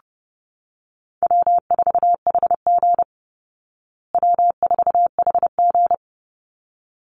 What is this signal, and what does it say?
Morse code, heard as a keyed tone, about 28 wpm, 700 Hz. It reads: W4HG W4HG
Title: Call Signs Repeated 3x - 30wpm